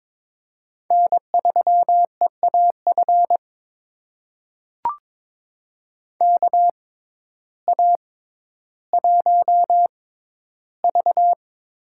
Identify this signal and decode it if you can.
Morse code, heard as a keyed tone, about 22 wpm, spaced as for 12 wpm, 700 Hz. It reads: N3EAF KA1V
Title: USA Call Signs - 6x Character Spacing - 22wpm